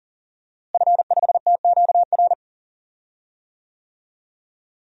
Morse code, heard as a keyed tone, about 40 wpm, 700 Hz. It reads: F5TQR
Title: DXCC-100 Call Signs - 5 Characters - Repeated 3x - 40wpm